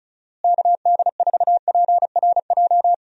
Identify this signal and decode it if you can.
Morse code, heard as a keyed tone, about 34 wpm, 700 Hz. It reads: KD4PRJ
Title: USA Call Signs - 6 Characters - Repeated 2x - 35wpm